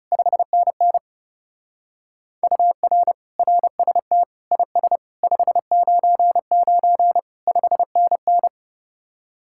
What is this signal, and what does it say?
Morse code, heard as a keyed tone, about 39 wpm, 700 Hz. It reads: 5NN UR RST IS 599 5NN